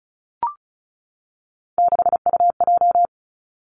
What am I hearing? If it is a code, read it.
Morse code, heard as a keyed tone, about 36 wpm, 700 Hz. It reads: 6UJ
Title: Three Challenging Letter-Number Combinations 35wpm